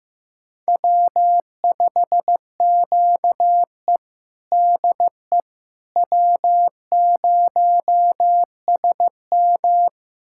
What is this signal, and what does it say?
Morse code, heard as a keyed tone, about 16 wpm, 700 Hz. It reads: W5QE DE W0SM
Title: USA Callsign DE Callsign - 4 Characters - 15wpm